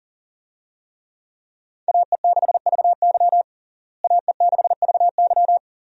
Morse code, heard as a keyed tone, about 40 wpm, 700 Hz. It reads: AE6VY AE6VY
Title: USA Call Signs - 5 Characters - Repeated 3x - 40wpm